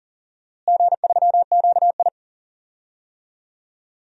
Morse code, heard as a keyed tone, about 40 wpm, 700 Hz. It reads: G3QI
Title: DXCC-100 Call Signs - 4 Characters - Repeated 3x - 40wpm